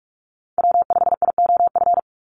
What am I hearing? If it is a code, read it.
Morse code, heard as a keyed tone, about 46 wpm, 700 Hz. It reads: W5IOF
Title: USA Call Signs 45wpm